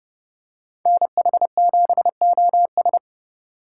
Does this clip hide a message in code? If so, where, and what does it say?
Yes: Morse code from 0.9 to 3.0 s, heard as a keyed tone, about 29 wpm, 700 Hz. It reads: NH7OS